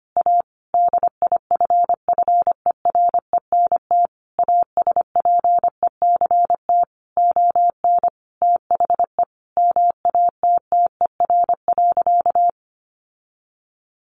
Morse code, heard as a keyed tone, about 25 wpm, 700 Hz. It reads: A DIFFERENT ASPECT ON THE MATTER.
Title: Sentences from Top 1300 Words 25wpm